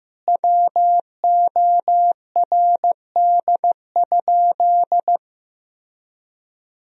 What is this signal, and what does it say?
Morse code, heard as a keyed tone, about 15 wpm, 700 Hz. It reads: WORD?
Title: Sentences from Top 1000 Words - Review Part 4 - 15wpm